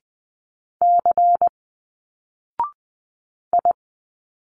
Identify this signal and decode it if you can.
Morse code, heard as a keyed tone, about 20 wpm, 700 Hz. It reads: C I